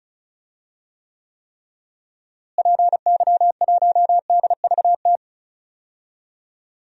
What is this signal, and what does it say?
Morse code, heard as a keyed tone, about 34 wpm, 700 Hz. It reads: PY1DVT
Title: DXCC-100 Call Signs - 6 Characters - Repeated 3x - 35wpm